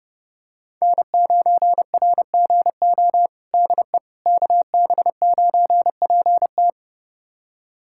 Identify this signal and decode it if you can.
Morse code, heard as a keyed tone, about 30 wpm, 700 Hz. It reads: N9RGO DE KB9PT
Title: USA Callsign DE Callsign - 5 Characters - 30wpm